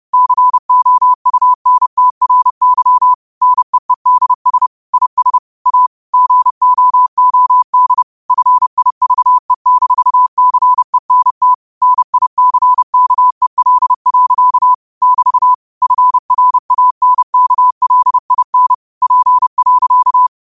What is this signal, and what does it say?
Morse code, heard as a keyed tone, about 30 wpm, 1000 Hz. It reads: COUNTRY NEEDS IS A GOOD FIVE-CENT NICKEL. <BT> FRANKLIN P.
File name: qotdfiles / 30wpm